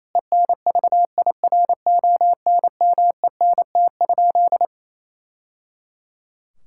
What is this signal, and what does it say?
Morse code, heard as a keyed tone, about 28 wpm, 700 Hz. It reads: ENVIRONMENT?
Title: Sentences from Top 1000 Words - Review Part 5 - 28wpm